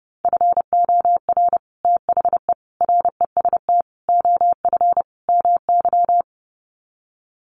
Morse code, heard as a keyed tone, about 30 wpm, 700 Hz. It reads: FOR THE REST OF MY
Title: Sets of 5 Words 30wpm